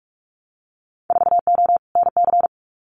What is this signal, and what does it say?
Morse code, heard as a keyed tone, about 46 wpm, 700 Hz. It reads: 4O NC
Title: Winter Field Day Exchanges Encore 45wpm